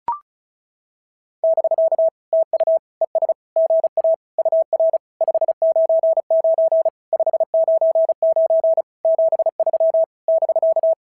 Morse code, heard as a keyed tone, about 36 wpm, 690 Hz. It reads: <BK> TU ES GA UR 599 599 73 <BK>
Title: SOTA QSO Practice 35wpm